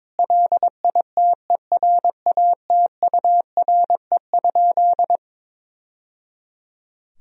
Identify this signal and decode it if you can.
Morse code, heard as a keyed tone, about 21 wpm, 700 Hz. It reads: LITERATURE?
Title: Sentences from Top 1000 Words - Review Part 3 - 22wpm